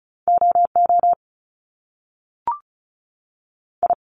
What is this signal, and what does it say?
Morse code, heard as a keyed tone, about 35 wpm, 700 Hz. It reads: O O I
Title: Top Two Letter Combinations in Words 35wpm